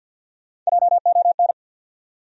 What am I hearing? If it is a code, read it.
Morse code, heard as a keyed tone, about 54 wpm, 700 Hz. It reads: JON
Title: USA Common Names - Max 3 Letters - 50wpm